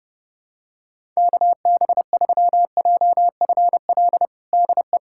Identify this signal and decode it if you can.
Morse code, heard as a keyed tone, about 30 wpm, 700 Hz. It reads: KB3JFL DE
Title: USA Callsign DE Callsign 30wpm